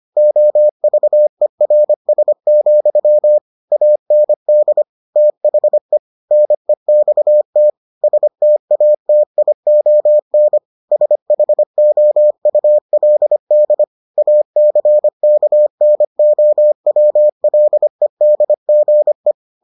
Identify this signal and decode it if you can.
Morse code, heard as a keyed tone, about 25 wpm, 600 Hz. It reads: OVERS, AND THE NEXT STATION SHOULD ACKNOWLEDGE